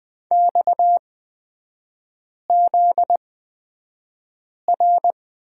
Morse code, heard as a keyed tone, about 20 wpm, spaced as for 5 wpm, 700 Hz. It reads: XZR
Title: 243 Lesson 39e - Letter z - Three Characters 20wpm